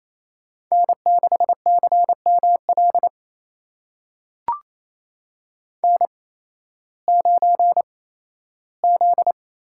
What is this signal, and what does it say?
Morse code, heard as a keyed tone, about 28 wpm, spaced as for 15 wpm, 700 Hz. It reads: N6CML N9Z
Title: USA Call Signs - 5 Characters - 8x Character Spacing - 28wpm